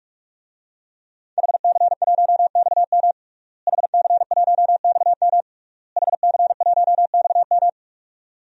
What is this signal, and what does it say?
Morse code, heard as a keyed tone, about 41 wpm, 700 Hz. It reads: HC1XM HC1XM HC1XM
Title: DXCC-100 Call Signs - 5 Characters - Repeated 3x - 45wpm